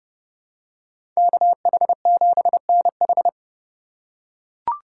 Morse code, heard as a keyed tone, about 30 wpm, 700 Hz. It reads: KH7NH
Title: USA Call Signs - 2x Character Spacing - 30wpm